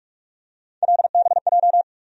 Morse code, heard as a keyed tone, about 45 wpm, 700 Hz. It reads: LBJ
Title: USA Call Sign Postfixes - 3 Letters 45wpm